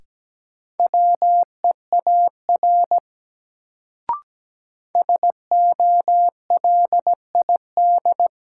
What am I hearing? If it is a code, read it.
Morse code, heard as a keyed tone, about 17 wpm, 700 Hz. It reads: WEAR SOLID
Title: Sentences from Top 1500 Words - New Word Drill Alt - 17wpm